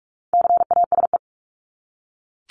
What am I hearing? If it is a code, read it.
Morse code, heard as a keyed tone, about 44 wpm, 700 Hz. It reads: CASE
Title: ES:S Suffix - Top 100 Words with Base Words 45wpm